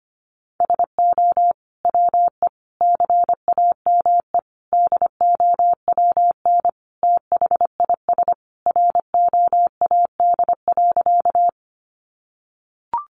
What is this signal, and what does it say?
Morse code, heard as a keyed tone, about 25 wpm, 700 Hz. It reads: SO WE CAME DOWN THIS ROAD.
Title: Sentences from Top 400 Words 25wpm